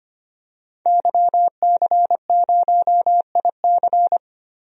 Morse code, heard as a keyed tone, about 25 wpm, 700 Hz. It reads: YC0IC
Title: DXCC-100 Call Signs - 5 Characters - 25wpm